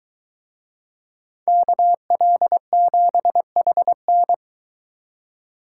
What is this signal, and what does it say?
Morse code, heard as a keyed tone, about 22 wpm, 700 Hz. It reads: KL7HN